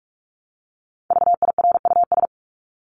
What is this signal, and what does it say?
Morse code, heard as a keyed tone, about 44 wpm, 700 Hz. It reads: VIRUS
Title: Top 1251-1500 Words 45wpm